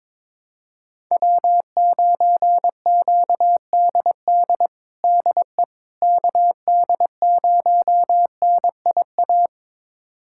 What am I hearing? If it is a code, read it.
Morse code, heard as a keyed tone, about 22 wpm, 700 Hz. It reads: W9QDD DE KD0NIA
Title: USA Callsign DE Callsign 22wpm